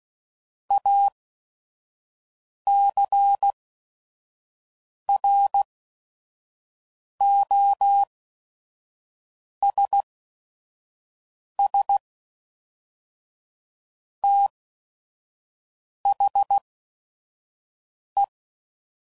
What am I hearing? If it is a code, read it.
Morse code, heard as a keyed tone, about 16 wpm, spaced as for 5 wpm, 800 Hz. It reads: ACROSS THE
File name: CW Code Groups @5WPM / Technical Subjects / XC